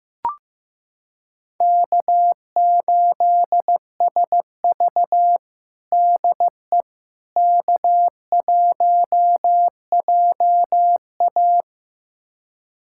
Morse code, heard as a keyed tone, about 15 wpm, 700 Hz. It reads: K8SV DE K1JA
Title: USA Callsign DE Callsign - 4 Characters - 15wpm